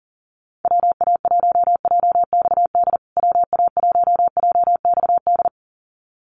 Morse code, heard as a keyed tone, about 40 wpm, 700 Hz. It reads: WA1JXD WA1JXD
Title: USA Call Signs Repeated 2x 40wpm